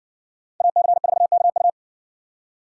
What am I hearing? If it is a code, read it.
Morse code, heard as a keyed tone, about 60 wpm, 700 Hz. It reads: AC4KU